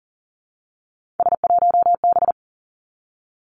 Morse code, heard as a keyed tone, about 38 wpm, 700 Hz. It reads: S1B
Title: Three Challenging Letter-Number Combinations 40wpm